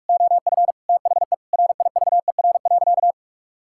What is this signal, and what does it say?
Morse code, heard as a keyed tone, about 41 wpm, 700 Hz. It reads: OF THE RIVER.
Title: Sentences from Top 1000 Words - Review Part 6 - 45wpm